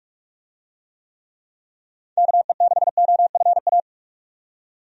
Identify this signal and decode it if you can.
Morse code, heard as a keyed tone, about 46 wpm, 700 Hz. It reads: KE6OFA
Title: USA Call Signs - 6 Characters - 45wpm